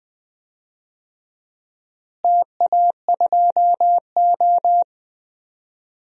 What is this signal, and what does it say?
Morse code, heard as a keyed tone, about 20 wpm, 700 Hz. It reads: TA2O
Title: DXCC-100 Call Signs - 4 Characters - Repeated 3x - 20wpm